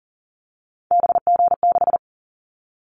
Morse code, heard as a keyed tone, about 38 wpm, 690 Hz. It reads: BG6
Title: Callsign Prefixes for Top 100 DXCC - 40wpm